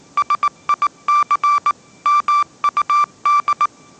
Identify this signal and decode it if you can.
Morse code, heard as a keyed tone, about 19 wpm, 1170 Hz. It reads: SIC MUD
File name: 20wpm